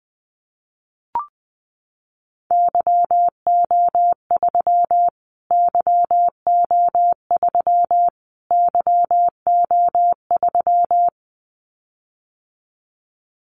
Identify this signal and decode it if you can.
Morse code, heard as a keyed tone, about 20 wpm, 700 Hz. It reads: YO3 YO3 YO3
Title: Callsign Prefixes for Top 25 DXCC - Repeated 3x - 20wpm